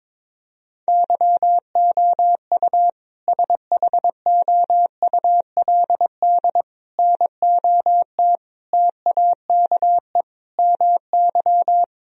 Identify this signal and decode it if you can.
Morse code, heard as a keyed tone, about 21 wpm, 700 Hz. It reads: YOU SHOULD NOT TAKE MY
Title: Sentences from Top 1000 Words - Review Part 3 - 22wpm